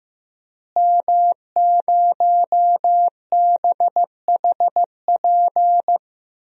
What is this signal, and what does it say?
Morse code, heard as a keyed tone, about 15 wpm, 700 Hz. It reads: M0BHP
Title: DXCC-100 Call Signs - 5 Characters - Repeated 3x - 15wpm